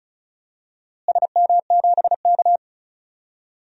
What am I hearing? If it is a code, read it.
Morse code, heard as a keyed tone, about 36 wpm, 700 Hz. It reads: SM7K
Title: DXCC-100 Call Signs - 4 Characters - Repeated 2x - 35wpm